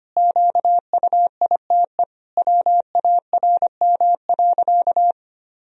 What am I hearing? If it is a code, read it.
Morse code, heard as a keyed tone, about 25 wpm, 700 Hz. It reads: QUITE WARM.
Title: Sentences from Top 1000 Words - Review Part 2 - 25wpm